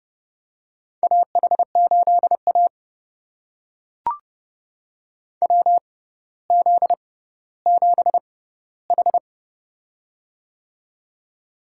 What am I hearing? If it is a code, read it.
Morse code, heard as a keyed tone, about 30 wpm, spaced as for 15 wpm, 700 Hz. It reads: AH8A WZ7H
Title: USA Call Signs - 4 Characters - 6x Character Spacing - 30wpm